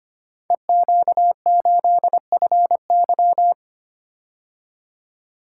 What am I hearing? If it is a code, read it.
Morse code, heard as a keyed tone, about 25 wpm, 700 Hz. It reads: EQ8FY
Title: Code Groups - 5 Letter:Numbers - 25wpm